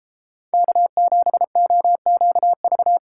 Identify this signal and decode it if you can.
Morse code, heard as a keyed tone, about 33 wpm, 700 Hz. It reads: K7OQV